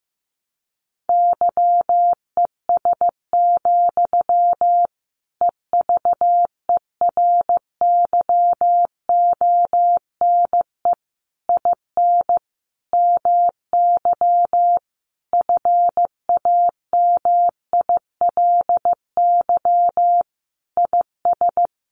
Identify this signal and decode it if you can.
Morse code, heard as a keyed tone, about 14 wpm, 700 Hz. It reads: YES, EVERYONE IN MY FAMILY IS
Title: Sentences from Top 800 Words 15wpm